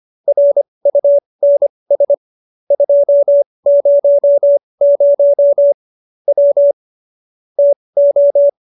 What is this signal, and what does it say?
Morse code, heard as a keyed tone, about 25 wpm, spaced as for 20 wpm, 570 Hz. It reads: RUNS 200 W TO